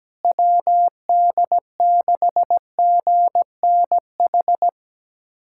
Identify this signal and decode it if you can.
Morse code, heard as a keyed tone, about 17 wpm, 700 Hz. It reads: WD6GNH
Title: USA Callsign DE Callsign 17wpm